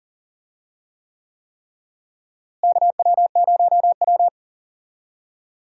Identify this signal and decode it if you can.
Morse code, heard as a keyed tone, about 40 wpm, 700 Hz. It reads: KW0W
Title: USA Call Signs - 4 Characters - Repeated 3x - 40wpm